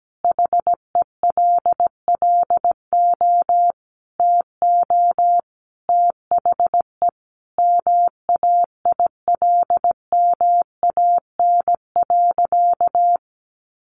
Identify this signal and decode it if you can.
Morse code, heard as a keyed tone, about 17 wpm, 700 Hz. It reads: HELLO TO THE MAILMAN.
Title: Easy Phrases 17wpm